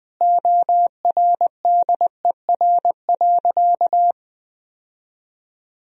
Morse code, heard as a keyed tone, about 20 wpm, 700 Hz. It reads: ORDER.
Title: Sentences from Top 1000 Words - Review Part 2 - 20wpm